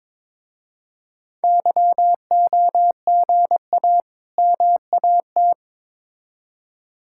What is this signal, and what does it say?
Morse code, heard as a keyed tone, about 21 wpm, 700 Hz. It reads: YOGA MAT
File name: Difficult Words with Hints - 22wpm